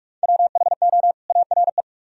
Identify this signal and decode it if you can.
Morse code, heard as a keyed tone, about 42 wpm, 700 Hz. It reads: WHO ARE